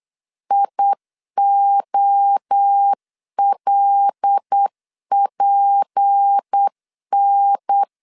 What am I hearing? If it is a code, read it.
Morse code, heard as a keyed tone, about 8 wpm, 800 Hz. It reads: IOLPN
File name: CW / DET / 1 6 1500C S C 8wpm 5219